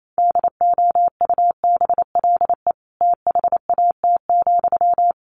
Morse code, heard as a keyed tone, about 28 wpm, 700 Hz. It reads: DOUBLE THAT,
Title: Sentences from Top 1400 Words 28wpm